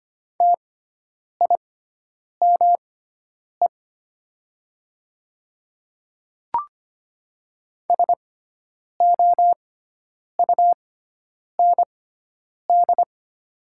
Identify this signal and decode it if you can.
Morse code, heard as a keyed tone, about 25 wpm, spaced as for 9 wpm, 700 Hz. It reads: TIME SOUND
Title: 6x Character Spacing - Top 100 Words - 25wpm